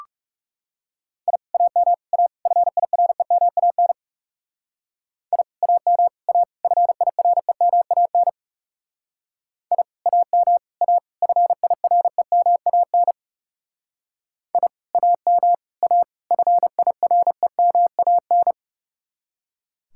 Morse code, heard as a keyed tone, about 37 wpm, 700 Hz. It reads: I AM A FIREMAN I AM A FIREMAN I AM A FIREMAN I AM A FIREMAN
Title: Speed-Racing Common QSO Phrases - 30wpm